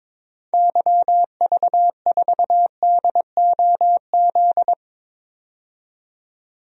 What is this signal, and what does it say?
Morse code, heard as a keyed tone, about 22 wpm, 700 Hz. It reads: YV4DOZ
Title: DXCC-100 Call Signs - 6 Characters - 22wpm